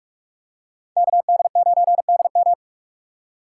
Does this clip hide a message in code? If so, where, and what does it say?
Yes: Morse code from 1.0 to 2.5 s, heard as a keyed tone, about 44 wpm, 700 Hz. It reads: KD9DM